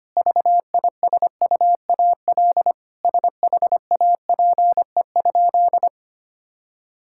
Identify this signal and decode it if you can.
Morse code, heard as a keyed tone, about 25 wpm, 700 Hz. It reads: VISUAL SHAPE?
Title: Sentences from Top 1000 Words - Review Part 6 - 25wpm